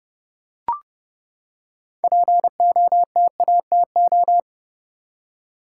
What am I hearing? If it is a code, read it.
Morse code, heard as a keyed tone, about 30 wpm, 700 Hz. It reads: POTATO
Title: Six Letter Words - 30wpm